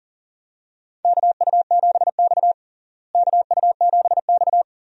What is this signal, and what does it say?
Morse code, heard as a keyed tone, about 40 wpm, 700 Hz. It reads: KU7X KU7X
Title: USA Call Signs - 4 Characters - Repeated 2x - 40wpm